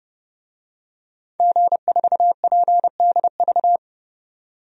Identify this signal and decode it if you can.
Morse code, heard as a keyed tone, about 30 wpm, 700 Hz. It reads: G4PDV
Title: DXCC-100 Call Signs - 5 Characters - 30wpm